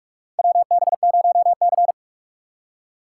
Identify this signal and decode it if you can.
Morse code, heard as a keyed tone, about 46 wpm, 700 Hz. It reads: WB0C
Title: USA Call Signs - 4 Characters - 3x Character Spacing - 45wpm